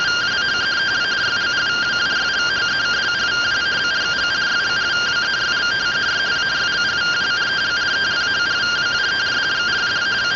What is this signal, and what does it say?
Two-tone teleprinter signal, about 170 Hz shift.